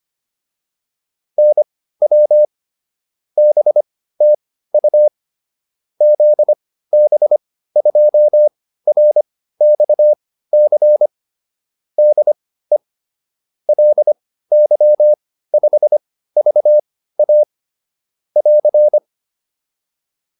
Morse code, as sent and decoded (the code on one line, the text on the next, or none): -. .-- / -... - ..- / --.. -... ..--- .-. -..- -.-. / -.. / .-.. -.-- ..... ...- .- / .-.-.
NW BTU ZB2RXC DE LY5VA <AR>